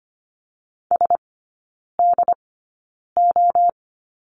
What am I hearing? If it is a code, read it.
Morse code, heard as a keyed tone, about 25 wpm, spaced as for 9 wpm, 700 Hz. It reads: SDO
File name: Lesson 15e - Number 5 - Three Characters 25wpm